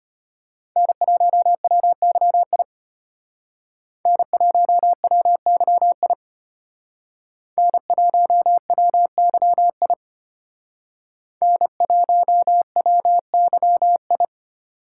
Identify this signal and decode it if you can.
Morse code, heard as a keyed tone, about 32 wpm, 700 Hz. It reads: N1WYI N1WYI N1WYI N1WYI